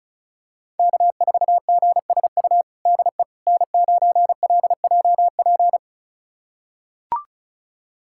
Morse code, heard as a keyed tone, about 34 wpm, 700 Hz. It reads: K4GSU DE N9LJP
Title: USA Callsign DE Callsign 35wpm